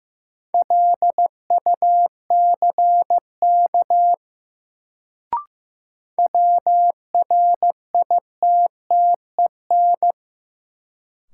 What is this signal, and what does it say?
Morse code, heard as a keyed tone, about 15 wpm, 700 Hz. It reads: LUCK WRITTEN
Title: Sentences from Top 1500 Words - New Word Drill - 15wpm